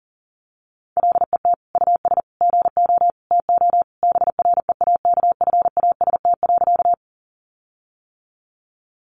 Morse code, heard as a keyed tone, about 40 wpm, 700 Hz. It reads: LET US GO TO BREAKFAST.
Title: Sentences from Top 1500 Words 40wpm